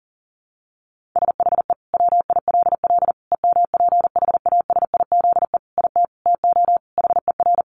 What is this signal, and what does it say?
Morse code, heard as a keyed tone, about 40 wpm, 700 Hz. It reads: SHE WILL EMPHASIZE IT TO HER
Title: Sentences from Top 1500 Words 40wpm